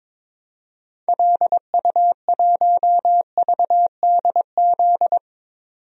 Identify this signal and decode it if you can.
Morse code, heard as a keyed tone, about 21 wpm, 700 Hz. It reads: LU1VDZ
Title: DXCC-100 Call Signs - 6 Characters - 22wpm